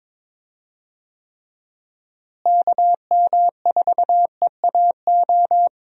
Morse code, heard as a keyed tone, about 22 wpm, 700 Hz. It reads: KM4EAO
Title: USA Call Signs - 6 Characters - Repeated 3x - 22wpm